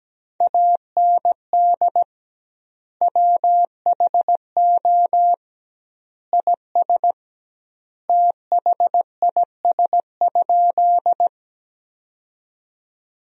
Morse code, heard as a keyed tone, about 17 wpm, 700 Hz. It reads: AND WHO IS THIS?
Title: 2x Word-Spacing - Sentences from Top 100 Words - 17wpm